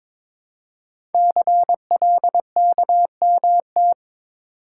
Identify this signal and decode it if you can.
Morse code, heard as a keyed tone, about 22 wpm, 700 Hz. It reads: CLKMT
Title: Code Groups - 5 Letters - 22wpm